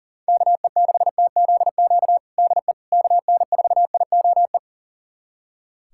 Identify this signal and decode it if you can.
Morse code, heard as a keyed tone, about 37 wpm, 700 Hz. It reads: KE6TZQ DE KN4IOE
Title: USA Callsign DE Callsign 40wpm